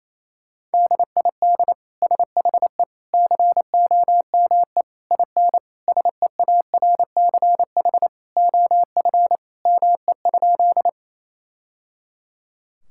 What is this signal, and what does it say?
Morse code, heard as a keyed tone, about 28 wpm, 700 Hz. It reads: DID SHE COME IN SEARCH OF ME?
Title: Sentences from Top 1000 Words - Review Part 5 - 28wpm